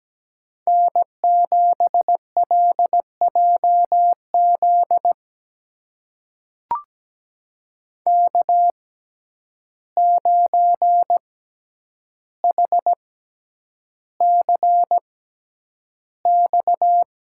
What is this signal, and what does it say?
Morse code, heard as a keyed tone, about 17 wpm, spaced as for 9 wpm, 700 Hz. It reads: N7LJZ K9HCX
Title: USA Call Signs - 5 Characters - 6x Character Spacing - 17wpm